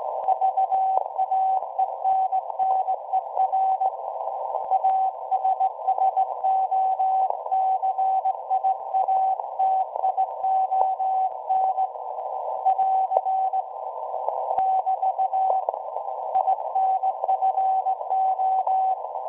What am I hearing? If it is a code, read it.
Morse code, heard as a keyed tone, about 18 wpm, 760 Hz. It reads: VAENNER ASSOCIATION <AR> <BT> INFO
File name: WebSDR records / SAQ at PI4THT part od 1005z 17-2kHz